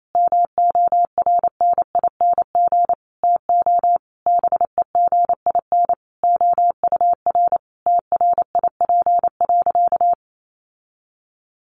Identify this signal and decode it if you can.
Morse code, heard as a keyed tone, about 28 wpm, 700 Hz. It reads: MORNING TO BEGIN OUR TRIP.
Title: Sentences from Top 1300 Words 28wpm